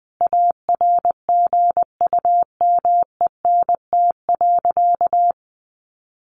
Morse code, heard as a keyed tone, about 19 wpm, 700 Hz. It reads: ARGUMENT.
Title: Sentences from Top 1400 Words 20wpm